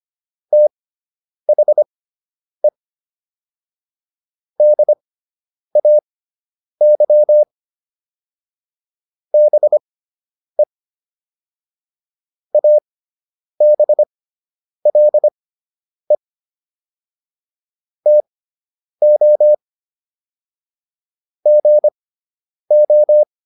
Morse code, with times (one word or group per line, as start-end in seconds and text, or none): 0.5-2.7 THE
4.6-7.4 DAY
9.3-10.6 BE
12.5-16.2 ABLE
18.1-19.5 TO
21.5-23.2 GO